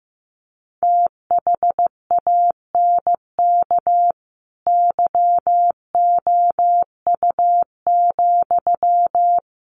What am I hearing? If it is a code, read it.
Morse code, heard as a keyed tone, about 15 wpm, 700 Hz. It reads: THANK YOU,
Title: Sentences from Top 1200 Words 15wpm